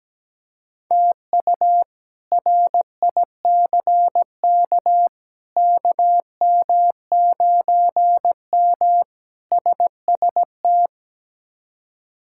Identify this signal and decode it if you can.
Morse code, heard as a keyed tone, about 17 wpm, 700 Hz. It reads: TU RICK KM9M SST
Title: K1USN SST QSO Practice - 17wpm